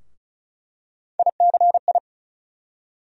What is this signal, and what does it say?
Morse code, heard as a keyed tone, about 34 wpm, 700 Hz. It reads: ICI
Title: USA Call Sign Postfixes - 3 Letters 35wpm